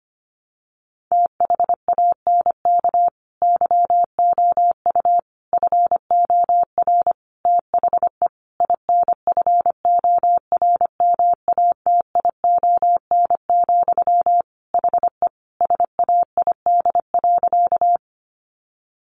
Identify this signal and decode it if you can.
Morse code, heard as a keyed tone, about 25 wpm, 700 Hz. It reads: THANK YOU FOR THE INFORMATION, HE SAID.
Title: Sentences from Top 600 Words 25wpm